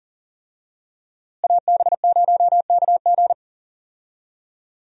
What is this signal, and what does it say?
Morse code, heard as a keyed tone, about 40 wpm, 700 Hz. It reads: AB0KG
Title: USA Call Signs - 5 Characters - Repeated 3x - 40wpm